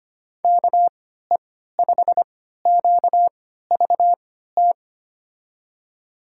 Morse code, heard as a keyed tone, about 25 wpm, spaced as for 14 wpm, 700 Hz. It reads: KE5QVT
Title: USA Call Signs - 6 Characters - 3x Character Spacing - 25wpm